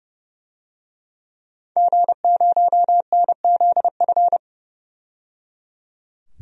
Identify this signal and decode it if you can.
Morse code, heard as a keyed tone, about 29 wpm, 700 Hz. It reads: G0NZF
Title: DXCC-100 Call Signs - 5 Characters - 30wpm